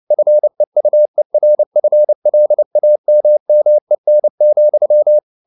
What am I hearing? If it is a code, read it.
Morse code, heard as a keyed tone, about 29 wpm, 600 Hz. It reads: FEUERFLAMMEN,